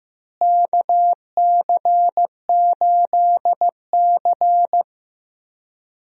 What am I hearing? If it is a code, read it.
Morse code, heard as a keyed tone, about 16 wpm, 700 Hz. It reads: KC8C
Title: USA Call Signs - 4 Characters - Repeated 3x - 15wpm